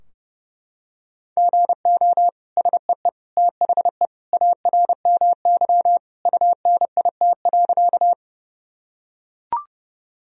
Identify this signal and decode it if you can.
Morse code, heard as a keyed tone, about 30 wpm, 700 Hz. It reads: GO SEE THE ARMY UNIT.
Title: Sentences from Top 1000 Words - Review Part 2 - 30wpm